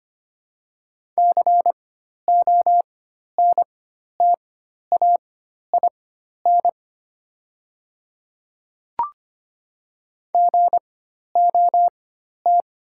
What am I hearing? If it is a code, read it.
Morse code, heard as a keyed tone, about 25 wpm, spaced as for 12 wpm, 700 Hz. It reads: CONTAIN GOT
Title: 4x Character Spacing - Top 500 Words - 25wpm